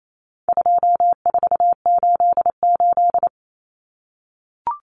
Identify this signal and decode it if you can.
Morse code, heard as a keyed tone, about 28 wpm, 700 Hz. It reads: 2488
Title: Common Cut Numbers - 4-Digit 28wpm